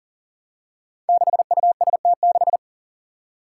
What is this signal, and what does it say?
Morse code, heard as a keyed tone, about 40 wpm, 700 Hz. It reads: 6UST6
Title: Five Challenging Letter-Number Combinations 40wpm